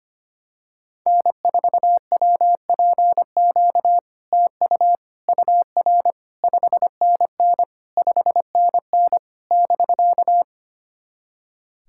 Morse code, heard as a keyed tone, about 25 wpm, 700 Hz. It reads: N4WPQ TU UR 5NN 5NN <BK>
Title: SOTA QSO Practice 25wpm